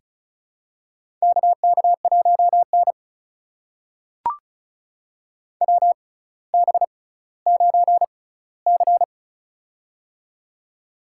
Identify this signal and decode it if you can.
Morse code, heard as a keyed tone, about 36 wpm, spaced as for 18 wpm, 700 Hz. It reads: KK1N WB9C
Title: USA Call Signs - 6x Character Spacing - 35wpm